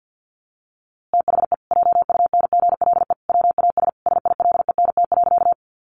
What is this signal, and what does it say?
Morse code, heard as a keyed tone, about 49 wpm, 700 Hz. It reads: THE JUNGLE WAS SILENT.
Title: A Sound Of Thunder 50wpm